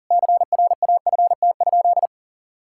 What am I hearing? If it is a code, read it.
Morse code, heard as a keyed tone, about 40 wpm, 700 Hz. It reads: CRAFT?
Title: Sentences from Top 1000 Words - Review Part 6 - 40wpm